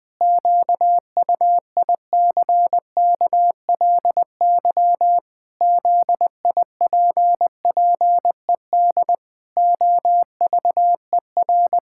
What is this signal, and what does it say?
Morse code, heard as a keyed tone, about 20 wpm, 700 Hz. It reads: QUICKLY ZIPPED OVER
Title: Pangram Challenge v2 - 20wpm